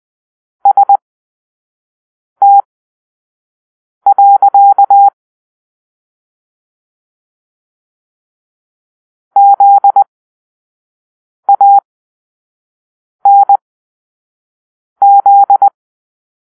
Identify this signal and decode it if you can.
Morse code, heard as a keyed tone, about 20 wpm, spaced as for 5 wpm, 800 Hz. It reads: ST. ZANZ
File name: arrl / Grp-7 Drill-3